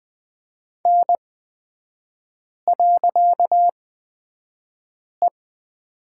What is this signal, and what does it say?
Morse code, heard as a keyed tone, about 20 wpm, spaced as for 5 wpm, 700 Hz. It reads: N.E
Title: 255 Lesson 41e - Period - Three Characters 20wpm